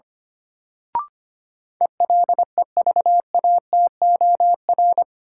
Morse code, heard as a keyed tone, about 25 wpm, 700 Hz. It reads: ELEVATOR
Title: Sentences from Top 1600 Words - New Word Drill - 25wpm